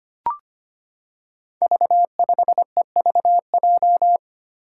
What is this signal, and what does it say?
Morse code, heard as a keyed tone, about 25 wpm, 700 Hz. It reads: V5EVJ
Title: Five Challenging Letter-Number Combinations 25wpm